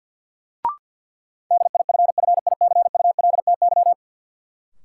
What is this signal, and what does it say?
Morse code, heard as a keyed tone, about 52 wpm, 700 Hz. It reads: DIFFICULTY
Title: Sentences from Top 1500 Words - New Word Drill - 50wpm